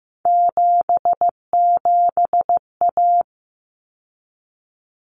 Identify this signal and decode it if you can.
Morse code, heard as a keyed tone, about 15 wpm, 700 Hz. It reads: 77A
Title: Common Cut Numbers - 3-Digit 15wpm